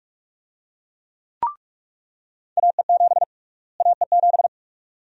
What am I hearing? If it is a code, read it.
Morse code, heard as a keyed tone, about 46 wpm, 700 Hz. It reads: AE7 AE7
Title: USA Call Sign Prefixes - Repeated 2x - 45wpm